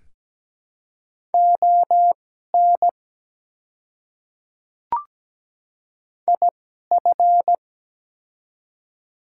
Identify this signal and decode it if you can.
Morse code, heard as a keyed tone, about 17 wpm, spaced as for 12 wpm, 700 Hz. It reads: ON IF
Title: 2x Character Spacing - Top 200 Words - 17wpm